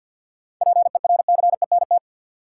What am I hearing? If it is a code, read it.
Morse code, heard as a keyed tone, about 52 wpm, 700 Hz. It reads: PERCENT
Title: Sets of 2 Words - New Word Drill - 50wpm